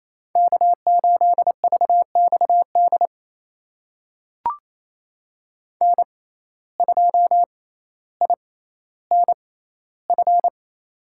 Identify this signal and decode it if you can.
Morse code, heard as a keyed tone, about 28 wpm, spaced as for 14 wpm, 700 Hz. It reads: K8VXD N2INF
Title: USA Call Signs - 5 Characters - 6x Character Spacing - 28wpm